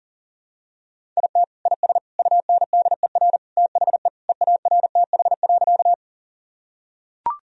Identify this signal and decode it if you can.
Morse code, heard as a keyed tone, about 37 wpm, 700 Hz. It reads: IT IS UNDER THE EARTH.
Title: Sentences from Top 1000 Words - Review Part 1 - 40wpm